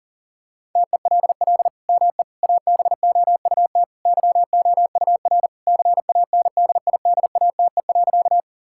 Morse code, heard as a keyed tone, about 40 wpm, 700 Hz. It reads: TELL ME ABOUT YOUR CANDIDATE.
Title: Sentences from Top 1600 Words 40wpm